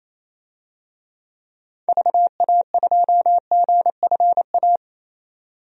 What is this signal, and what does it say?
Morse code, heard as a keyed tone, about 28 wpm, 700 Hz. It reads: VA2GFA
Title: DXCC-100 Call Signs - 6 Characters - 28wpm